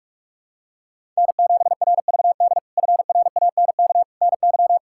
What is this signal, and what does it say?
Morse code, heard as a keyed tone, about 42 wpm, 700 Hz. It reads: N7RVD FRANK NY
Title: K1USN SST QSO Practice - 45wpm